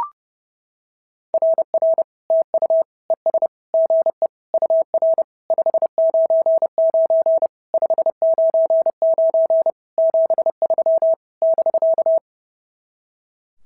Morse code, heard as a keyed tone, about 27 wpm, 660 Hz. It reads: RR TU ES GE UR 599 599 73 <BK>
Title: SOTA QSO Practice 30wpm